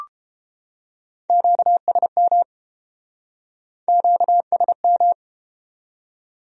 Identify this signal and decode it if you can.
Morse code, heard as a keyed tone, about 31 wpm, 700 Hz. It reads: QSM QSM